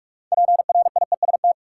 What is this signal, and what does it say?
Morse code, heard as a keyed tone, about 46 wpm, 700 Hz. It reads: PRIEST
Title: Six Letter Words - 45wpm